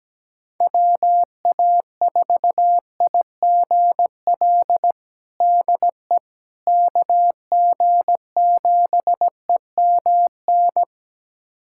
Morse code, heard as a keyed tone, about 17 wpm, 700 Hz. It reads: WA4IGL DE KG7EMN
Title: USA Callsign DE Callsign - 6 Characters - 17wpm